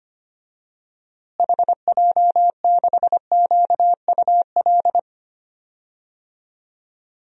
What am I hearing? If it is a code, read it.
Morse code, heard as a keyed tone, about 25 wpm, 700 Hz. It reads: HJ6QUL
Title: DXCC-100 Call Signs - 6 Characters - Repeated 3x - 25wpm